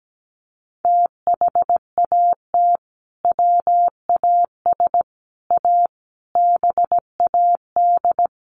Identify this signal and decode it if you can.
Morse code, heard as a keyed tone, about 17 wpm, 700 Hz. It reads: THAT WAS A BAD
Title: Sentences from Top 800 Words 17wpm